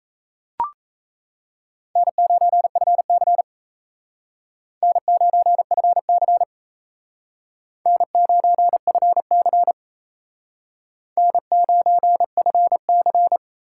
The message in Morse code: N9FC N9FC N9FC N9FC

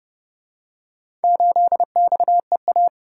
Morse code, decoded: 8XEA

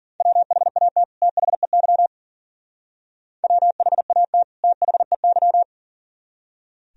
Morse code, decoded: WHAT THEY WHAT THEY

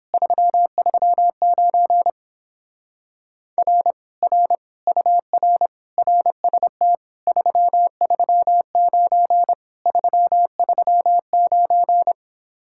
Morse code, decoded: 339 R R UR RST 339 339

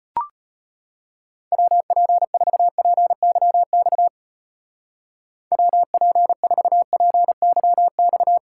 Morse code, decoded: WP4PYX WP4PYX

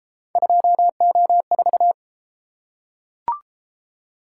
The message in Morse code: 2O4

